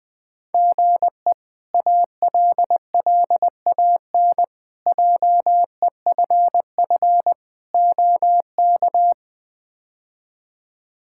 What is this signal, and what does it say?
Morse code, heard as a keyed tone, about 21 wpm, 700 Hz. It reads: GE ALLAN JEFF OK
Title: K1USN SST QSO Practice - 20wpm